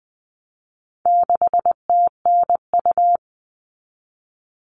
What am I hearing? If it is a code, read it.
Morse code, heard as a keyed tone, about 20 wpm, 700 Hz. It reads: 6TNU
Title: All Cut Numbers - 4-Digit 20wpm